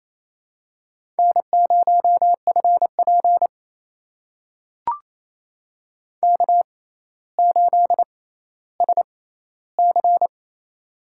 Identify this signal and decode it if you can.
Morse code, heard as a keyed tone, about 28 wpm, spaced as for 14 wpm, 700 Hz. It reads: N0FP K8SC
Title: USA Call Signs - 4 Characters - 6x Character Spacing - 28wpm